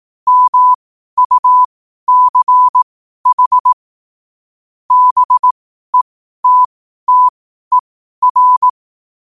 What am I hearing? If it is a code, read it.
Morse code, heard as a keyed tone, about 18 wpm, spaced as for 12 wpm, 1000 Hz. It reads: MUCH BETTER